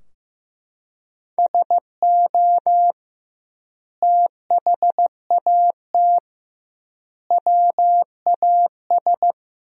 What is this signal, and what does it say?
Morse code, heard as a keyed tone, about 15 wpm, 700 Hz. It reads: SO THAT WAS
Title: 2x Word-Spacing - Sentences from Top 100 Words - 15wpm